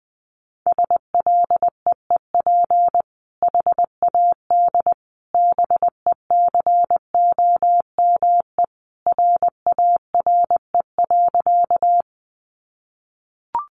Morse code, heard as a keyed tone, about 20 wpm, 700 Hz. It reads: SLEEP HAD BECOME RARE.
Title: Sentences from Top 1100 Words 20wpm